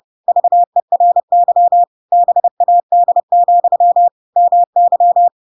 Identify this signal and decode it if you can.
Morse code, heard as a keyed tone, about 30 wpm, 700 Hz. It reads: VERY BAD, MY
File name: Pinocchio / Pinocchio ch3 30 30 ik1hge